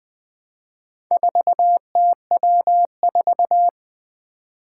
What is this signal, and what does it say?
Morse code, heard as a keyed tone, about 20 wpm, 700 Hz. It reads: 4TW4
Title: Four Challenging Letter-Number Combinations 20wpm